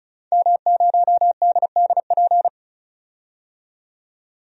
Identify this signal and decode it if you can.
Morse code, heard as a keyed tone, about 34 wpm, 700 Hz. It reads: M0DDP
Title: DXCC-100 Call Signs - 5 Characters - 35wpm